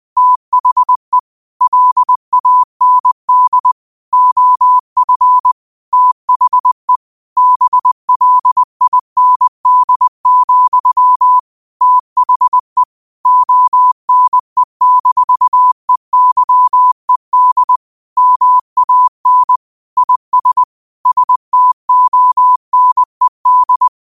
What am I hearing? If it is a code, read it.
Morse code, heard as a keyed tone, about 20 wpm, 1000 Hz. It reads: THE LAND OF THE BLIND, THE ONE-EYED MAN IS STONED
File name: qotdfiles / 20wpm